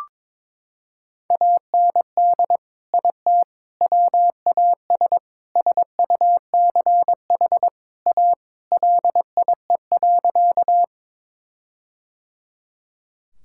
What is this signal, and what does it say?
Morse code, heard as a keyed tone, about 22 wpm, 700 Hz. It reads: AND IT WAS SUCH A LIE.
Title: Sentences from Top 1600 Words 22wpm